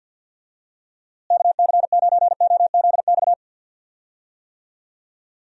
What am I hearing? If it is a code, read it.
Morse code, heard as a keyed tone, about 50 wpm, 700 Hz. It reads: KC9OZX
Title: USA Call Signs - 6 Characters - Repeated 2x - 50wpm